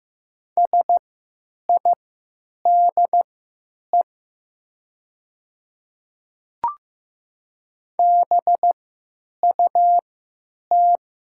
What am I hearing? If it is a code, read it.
Morse code, heard as a keyed tone, about 15 wpm, spaced as for 9 wpm, 700 Hz. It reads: SIDE BUT
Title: 3x Character Spacing - Top 100 Words - 15wpm